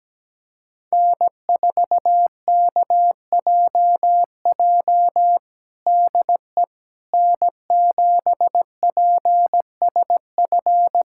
Morse code, heard as a keyed tone, about 17 wpm, 700 Hz. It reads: N4KJJ DE N7PSF
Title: USA Callsign DE Callsign - 5 Characters - 17wpm